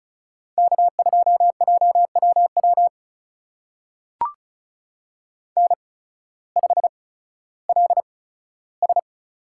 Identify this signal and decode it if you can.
Morse code, heard as a keyed tone, about 36 wpm, spaced as for 16 wpm, 700 Hz. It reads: K2JWW N5LS
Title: USA Call Signs - 5 Characters - 8x Character Spacing - 35wpm